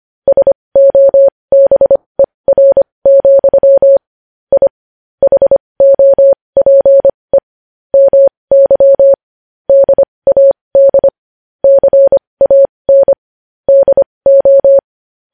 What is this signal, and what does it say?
Morse code, heard as a keyed tone, about 25 wpm, spaced as for 20 wpm, 550 Hz. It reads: SOBER, I HOPE MY DAD CAN DO
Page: News headlines in morse code at 20 wpm